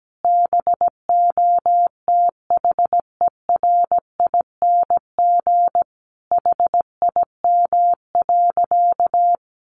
Morse code, heard as a keyed tone, about 17 wpm, 700 Hz. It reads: BOTHERING HIM.
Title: Additional Easy Phrases 17wpm